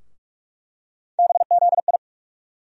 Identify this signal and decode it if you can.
Morse code, heard as a keyed tone, about 47 wpm, 700 Hz. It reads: BZI
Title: USA Call Sign Postfixes - 3 Letters 45wpm